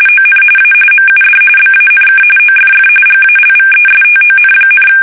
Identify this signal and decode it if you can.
RTTY, about 45.45 baud. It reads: ON A MUD BANK OFF GALVESTON,
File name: fsk850 / battleships